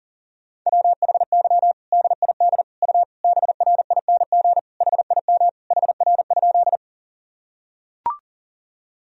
Morse code, heard as a keyed tone, about 40 wpm, 700 Hz. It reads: WHY DID U BRING HIM HR?
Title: Sentences from Top 500 Words and Abbreviations 40wpm